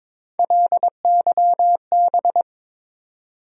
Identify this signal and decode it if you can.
Morse code, heard as a keyed tone, about 21 wpm, 700 Hz. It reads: LYB
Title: USA Call Sign Postfixes - 3 Letters 22wpm